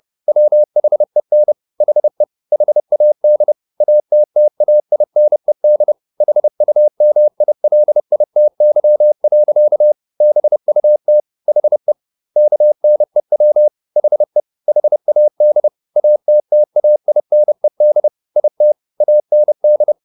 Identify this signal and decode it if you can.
Morse code, heard as a keyed tone, about 30 wpm, 600 Hz. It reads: WHEN HE HAD ATTAINED HUMILITY. BUT HE KNEW HE HAD ATTAINED IT AND
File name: the old man and the sea / the old man and the sea 1 30wpm